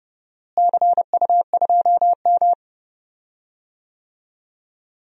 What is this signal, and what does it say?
Morse code, heard as a keyed tone, about 32 wpm, 700 Hz. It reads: CU2M
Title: DXCC-100 Call Signs - 4 Characters - Repeated 2x - 30wpm